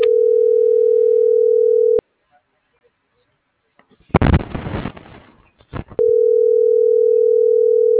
Ringback tone (North America), 440 + 480 Hz.
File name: CENTRAL MAYAB